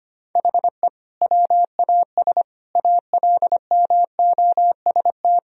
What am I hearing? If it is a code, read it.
Morse code, heard as a keyed tone, about 25 wpm, 700 Hz. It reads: HE WAS ALMOST
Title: Sentences from Top 1000 Words - Review Part 4 - 25wpm